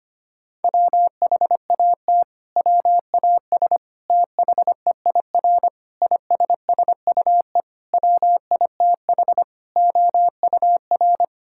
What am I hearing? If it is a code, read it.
Morse code, heard as a keyed tone, about 25 wpm, 700 Hz. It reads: WHAT WAS THEIR ISSUE WITH OUR
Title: Sentences from Top 1000 Words - Review Part 6 - 25wpm